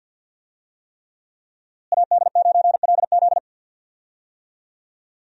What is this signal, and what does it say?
Morse code, heard as a keyed tone, about 46 wpm, 700 Hz. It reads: AD9LZ
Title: USA Call Signs - 5 Characters - Repeated 3x - 50wpm